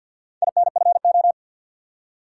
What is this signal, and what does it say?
Morse code, heard as a keyed tone, about 53 wpm, 700 Hz. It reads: INFO
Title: Top 180 QSO Elements 50wpm